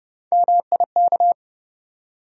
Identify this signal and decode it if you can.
Morse code, heard as a keyed tone, about 32 wpm, 700 Hz. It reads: MIK
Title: USA Call Sign Postfixes - 3 Letters 30wpm